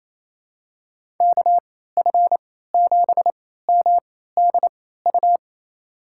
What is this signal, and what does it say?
Morse code, heard as a keyed tone, about 28 wpm, spaced as for 16 wpm, 700 Hz. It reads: KF7MDU
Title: USA Call Signs - 6 Characters - 3x Character Spacing - 28wpm